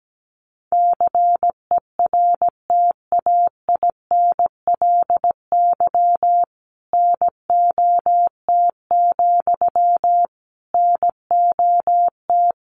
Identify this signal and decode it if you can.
Morse code, heard as a keyed tone, about 17 wpm, 700 Hz. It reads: CERTAINLY NOT, NOT
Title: Sentences from Top 900 Words 17wpm